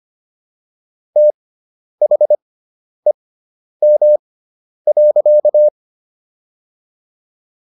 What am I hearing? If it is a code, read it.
Morse code, heard as a keyed tone, about 25 wpm, spaced as for 10 wpm, 600 Hz. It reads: THEM.